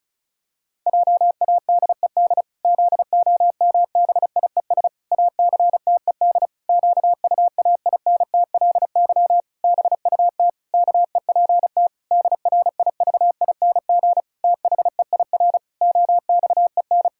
Morse code, heard as a keyed tone, about 34 wpm, 700 Hz. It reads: JADED ZOMBIES ACTED QUAINTLY BUT KEPT DRIVING THEIR OXEN
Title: Pangram Challenge v2 - 35wpm